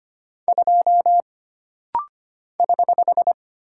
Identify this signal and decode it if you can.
Morse code, heard as a keyed tone, about 25 wpm, 700 Hz. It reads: 2 <HH>
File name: Single Character (Letter:Number:Punctuation:Prosign) - Rapid-Fire - 25wpm